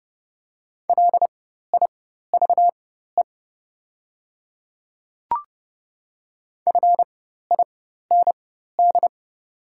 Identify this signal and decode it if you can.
Morse code, heard as a keyed tone, about 30 wpm, spaced as for 14 wpm, 700 Hz. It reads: LIVE FIND
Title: 4x Character Spacing - Top 200 Words - 30wpm